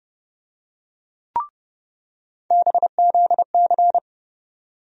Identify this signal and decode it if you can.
Morse code, heard as a keyed tone, about 30 wpm, 700 Hz. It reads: BZC
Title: USA Call Sign Postfixes - 3 Letters 30wpm